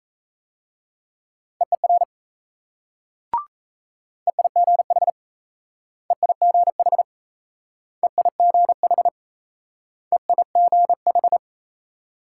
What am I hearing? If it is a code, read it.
Morse code, heard as a keyed tone, about 35 wpm, 700 Hz. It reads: EER EIGH EIGH EIGH EIGH
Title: Speed-Racing Multi-letter Phonemes - 28wpm